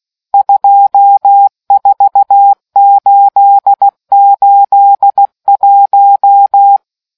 Morse code, heard as a keyed tone, about 16 wpm, 800 Hz. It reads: 24881